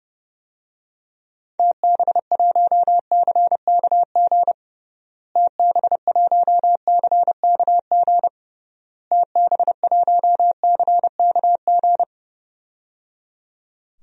Morse code, heard as a keyed tone, about 30 wpm, 700 Hz. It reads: TB1CKG TB1CKG TB1CKG
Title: DXCC-100 Call Signs - 6 Characters - Repeated 3x - 30wpm